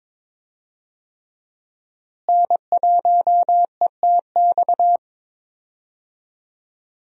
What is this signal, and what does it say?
Morse code, heard as a keyed tone, about 22 wpm, 700 Hz. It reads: N1ETX
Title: USA Call Signs - 5 Characters - Repeated 3x - 22wpm